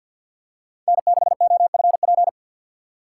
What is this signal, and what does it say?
Morse code, heard as a keyed tone, about 53 wpm, 690 Hz. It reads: N6OFP
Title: USA Call Signs - 5 Characters - 4x Character Spacing - 50wpm